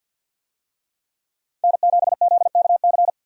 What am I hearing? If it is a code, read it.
Morse code, heard as a keyed tone, about 50 wpm, 700 Hz. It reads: N7ZKC
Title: USA Call Signs - 5 Characters - 50wpm